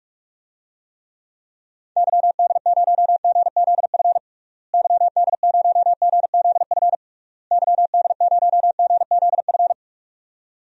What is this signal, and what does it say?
Morse code, heard as a keyed tone, about 42 wpm, 700 Hz. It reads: YD0GZF YD0GZF YD0GZF
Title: DXCC-100 Call Signs - 6 Characters - Repeated 3x - 45wpm